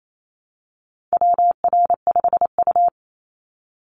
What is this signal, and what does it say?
Morse code, heard as a keyed tone, about 28 wpm, 700 Hz. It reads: WR5U
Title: Call Signs - 28wpm